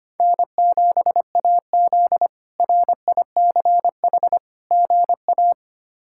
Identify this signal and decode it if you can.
Morse code, heard as a keyed tone, about 25 wpm, 700 Hz. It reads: N7AZ RICH GA
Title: K1USN SST QSO Practice - 25wpm